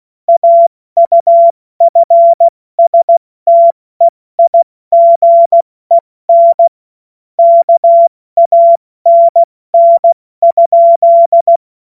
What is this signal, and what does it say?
Morse code, heard as a keyed tone, about 16 wpm, spaced as for 14 wpm, 680 Hz. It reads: AUFSTEIGEN KANN?
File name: Morsekurs / Witze / Witz 17.Oktober 2023 70 F80